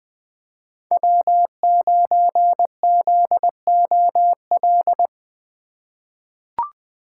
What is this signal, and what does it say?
Morse code, heard as a keyed tone, about 20 wpm, 700 Hz. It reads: W9ZOL
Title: USA Call Signs - 5 Characters - 4x Character Spacing - 20wpm